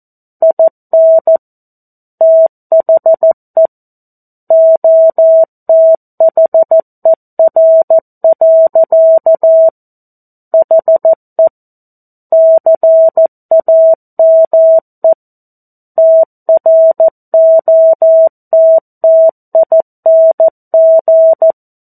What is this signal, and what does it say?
Morse code, heard as a keyed tone, about 14 wpm, 650 Hz. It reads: IN THE OTHER. HE CAME TROTTING
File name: 02-Chapter-02 14 WPM